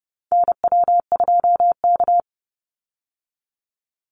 Morse code, heard as a keyed tone, about 30 wpm, 700 Hz. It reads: NW2K
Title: Call Signs Repeated 2x - 30wpm